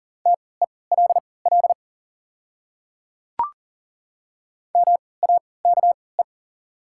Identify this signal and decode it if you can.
Morse code, heard as a keyed tone, about 39 wpm, spaced as for 22 wpm, 700 Hz. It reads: TELL MAKE
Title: 3x Character Spacing - Top 200 Words - 40wpm